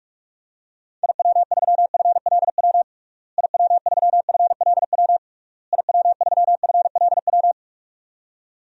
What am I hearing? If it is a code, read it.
Morse code, heard as a keyed tone, about 42 wpm, 700 Hz. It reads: IW3FLW IW3FLW IW3FLW
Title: DXCC-100 Call Signs - 6 Characters - Repeated 3x - 45wpm